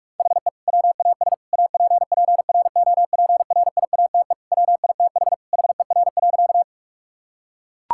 Morse code, heard as a keyed tone, about 41 wpm, 700 Hz. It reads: HE WAS APPROPRIATE WITH HER.
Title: Sentences from Top 1000 Words - Review Part 5 - 45wpm